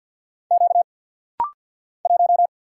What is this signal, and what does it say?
Morse code, heard as a keyed tone, about 48 wpm, 700 Hz. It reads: Q 1
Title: Single Letter-Number - Rapid-Fire 50wpm